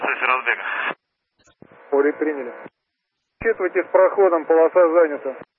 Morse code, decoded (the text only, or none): none